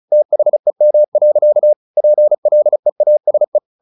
Morse code, heard as a keyed tone, about 34 wpm, 600 Hz. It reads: THEM. PLEASE
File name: the old man and the sea / the old man and the sea 1 35wpm